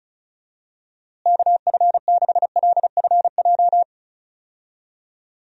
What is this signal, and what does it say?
Morse code, heard as a keyed tone, about 36 wpm, 700 Hz. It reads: KF6LFJ
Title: USA Call Signs - 6 Characters - Repeated 2x - 35wpm